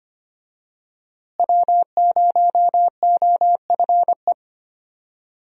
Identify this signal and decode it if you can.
Morse code, heard as a keyed tone, about 25 wpm, 700 Hz. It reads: W0OFE
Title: USA Call Signs - 5 Characters - Repeated 3x - 25wpm